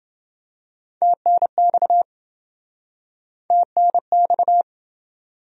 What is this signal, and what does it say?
Morse code, heard as a keyed tone, about 26 wpm, 700 Hz. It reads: TNX TNX